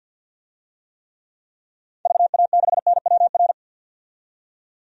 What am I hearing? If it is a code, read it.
Morse code, heard as a keyed tone, about 52 wpm, 700 Hz. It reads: VA6NWR
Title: DXCC-100 Call Signs - 6 Characters - 50wpm